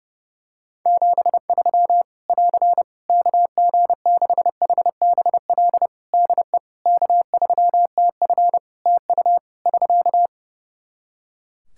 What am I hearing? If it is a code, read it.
Morse code, heard as a keyed tone, about 30 wpm, 700 Hz. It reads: 73 <AR> KG6HBL DE K3TF TU <SK>
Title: Practice QSO - Exchange 4 - Ending - 30wpm